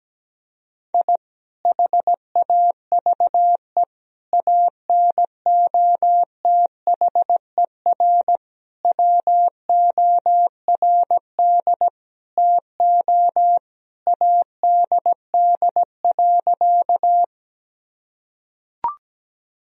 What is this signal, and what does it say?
Morse code, heard as a keyed tone, about 17 wpm, 700 Hz. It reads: I HAVE ANOTHER WORD TO ADD.
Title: Sentences from Top 1000 Words - Review Part 1 - 17wpm